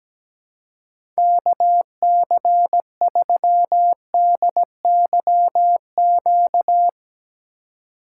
Morse code, heard as a keyed tone, about 17 wpm, 700 Hz. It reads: KC3DYQ
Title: USA Call Signs - 6 Characters - 3x Character Spacing - 17wpm